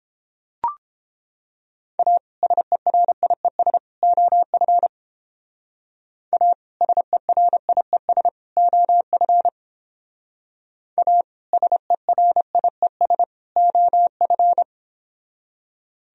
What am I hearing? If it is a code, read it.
Morse code, heard as a keyed tone, about 29 wpm, 700 Hz. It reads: A SERIES OF A SERIES OF A SERIES OF